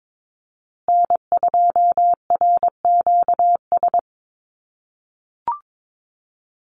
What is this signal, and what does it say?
Morse code, heard as a keyed tone, about 22 wpm, 700 Hz. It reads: N2RQS
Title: USA Call Signs Repeated 2x 22wpm